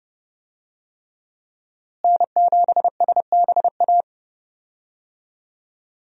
Morse code, heard as a keyed tone, about 30 wpm, 700 Hz. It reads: N7SBA
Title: USA Call Signs - 5 Characters - Repeated 3x - 30wpm